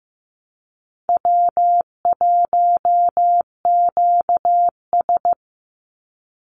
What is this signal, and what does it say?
Morse code, heard as a keyed tone, about 15 wpm, 700 Hz. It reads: W1QS
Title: Easy Call Signs - 15wpm